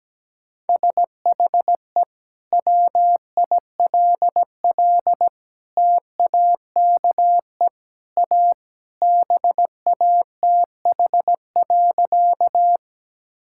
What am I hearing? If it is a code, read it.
Morse code, heard as a keyed tone, about 17 wpm, 700 Hz. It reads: SHE WILL TAKE A BATH.
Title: Sentences from Top 1600 Words 17wpm